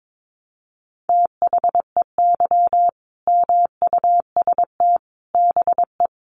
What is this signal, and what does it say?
Morse code, heard as a keyed tone, about 22 wpm, 700 Hz. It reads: THEY MUST BE
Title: Sentences from Top 1100 Words 22wpm